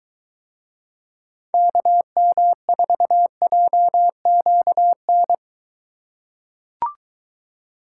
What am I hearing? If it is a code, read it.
Morse code, heard as a keyed tone, about 20 wpm, 700 Hz. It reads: KM4JQN